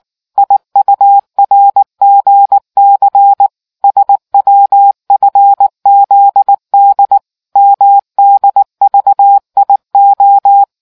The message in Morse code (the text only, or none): IURGC SWFZD MDVIO